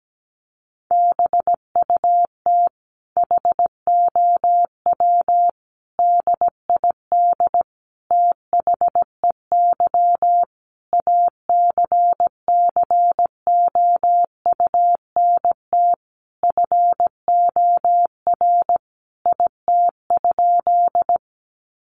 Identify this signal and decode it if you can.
Morse code, heard as a keyed tone, about 17 wpm, 700 Hz. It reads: BUT HOW DID THEY ACCOUNT FOR IT?
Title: Sentences from Top 900 Words 17wpm